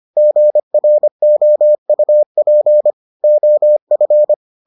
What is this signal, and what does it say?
Morse code, heard as a keyed tone, about 25 wpm, 600 Hz. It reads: GROUP OF